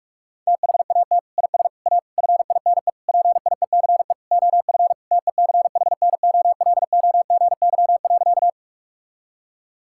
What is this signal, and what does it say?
Morse code, heard as a keyed tone, about 41 wpm, 700 Hz. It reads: THAT IS A FINE PIECE OF TECHNOLOGY.
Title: Sentences from Top 1000 Words - Review Part 5 - 45wpm